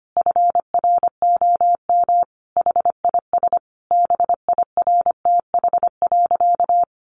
Morse code, heard as a keyed tone, about 25 wpm, 700 Hz. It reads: FROM HIS BIRTH.
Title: Sentences from Top 900 Words 25wpm